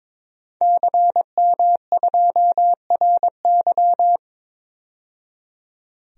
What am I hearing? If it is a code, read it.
Morse code, heard as a keyed tone, about 22 wpm, 700 Hz. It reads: CM2RY
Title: DXCC-100 Call Signs - 5 Characters - 22wpm